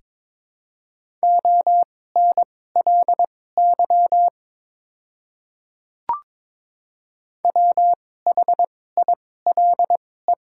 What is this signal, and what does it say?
Morse code, heard as a keyed tone, about 22 wpm, spaced as for 16 wpm, 700 Hz. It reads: ONLY WHILE
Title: 2x Character Spacing - Top 300 Words - 22wpm